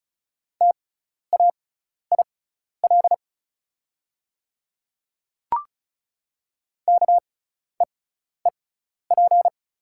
Morse code, heard as a keyed tone, about 36 wpm, spaced as for 12 wpm, 700 Hz. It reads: TAIL KEEP
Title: 6x Character Spacing - Top 500 Words - 35wpm